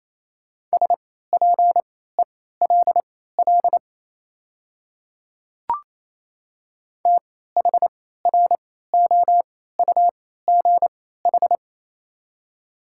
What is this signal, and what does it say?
Morse code, heard as a keyed tone, about 28 wpm, spaced as for 16 wpm, 700 Hz. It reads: SPELL THROUGH
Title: 3x Character Spacing - Top 300 Words - 28wpm